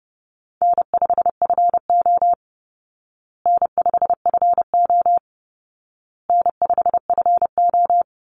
Morse code, heard as a keyed tone, about 30 wpm, 700 Hz. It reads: N5FO N5FO N5FO
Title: Call Signs Repeated 2x - 30wpm